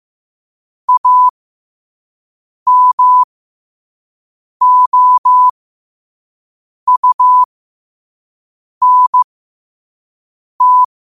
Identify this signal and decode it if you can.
Morse code, heard as a keyed tone, about 15 wpm, spaced as for 5 wpm, 1000 Hz. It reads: AMOUNT
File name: 5wpm